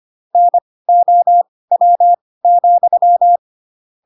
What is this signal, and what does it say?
Morse code, heard as a keyed tone, about 25 wpm, spaced as for 18 wpm, 700 Hz. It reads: NOW,
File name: Pinocchio / Pinocchio ch5 12 25 ik1hge